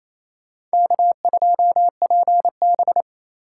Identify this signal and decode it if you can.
Morse code, heard as a keyed tone, about 35 wpm, 700 Hz. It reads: K2PB